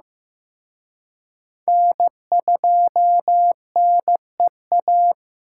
Morse code, heard as a keyed tone, about 18 wpm, 700 Hz. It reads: N2NEA
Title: Speed Racing - USA 1x3 Call Signs 15wpm